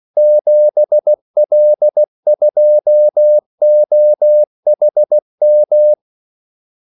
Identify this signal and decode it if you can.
Morse code, heard as a keyed tone, about 16 wpm, 600 Hz. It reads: 7L2OHM
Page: Kaldesignaler 16 wpm